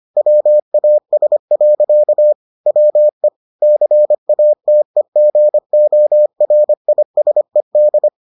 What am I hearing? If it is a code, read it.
Morse code, heard as a keyed tone, about 25 wpm, 600 Hz. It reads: WAS. WE CATEGORISED